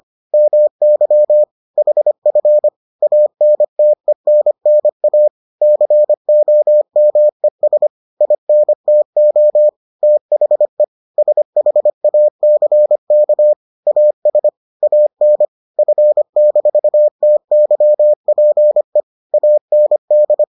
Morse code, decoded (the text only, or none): MY HF ANTENNA COMES INTO THE SHACK AS AN F-TYPE AND